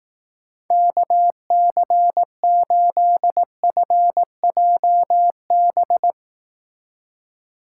Morse code, heard as a keyed tone, about 19 wpm, 700 Hz. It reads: KC8FJB
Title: Speed Racing - USA 2x3 Call Signs 15wpm